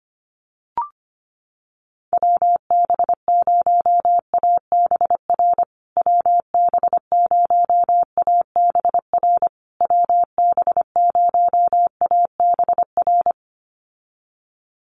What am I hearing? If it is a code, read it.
Morse code, heard as a keyed tone, about 25 wpm, 700 Hz. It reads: WB0ABR WB0ABR WB0ABR
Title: USA Call Signs Repeated 3x 25wpm